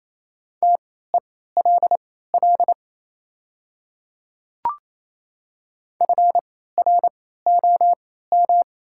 Morse code, heard as a keyed tone, about 27 wpm, spaced as for 16 wpm, 700 Hz. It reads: TELL FROM
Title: 3x Character Spacing - Top 200 Words - 28wpm